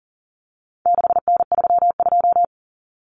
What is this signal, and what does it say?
Morse code, heard as a keyed tone, about 40 wpm, 700 Hz. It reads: 6N32
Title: Common Cut Numbers - 4-Digit 40wpm